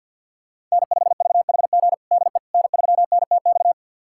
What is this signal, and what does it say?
Morse code, heard as a keyed tone, about 51 wpm, 700 Hz. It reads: N5VHG DE N3NTX
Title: USA Callsign DE Callsign - 5 Characters - 50wpm